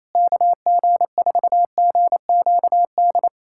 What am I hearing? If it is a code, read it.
Morse code, heard as a keyed tone, about 28 wpm, 700 Hz. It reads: KG4GQD
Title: USA Callsign DE Callsign - 6 Characters - 28wpm